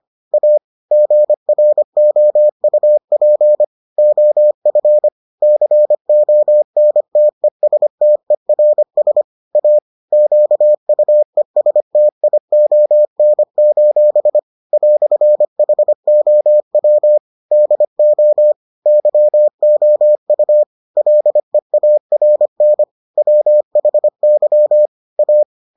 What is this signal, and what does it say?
Morse code, heard as a keyed tone, about 25 wpm, 600 Hz. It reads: A GROUP OF CONTESTERS A QUESTION: "HOW DO YOU LEARN WHY A